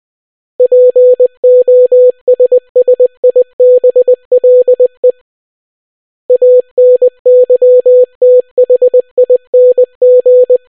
Morse code, heard as a keyed tone, about 20 wpm, 500 Hz. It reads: POSSIBLE ANYTHING